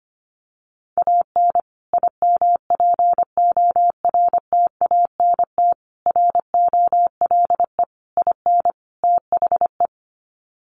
Morse code, heard as a keyed tone, about 25 wpm, 700 Hz. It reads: AN IMPORTANT ROLE IN THE
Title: Sets of 5 Words Encore2 25wpm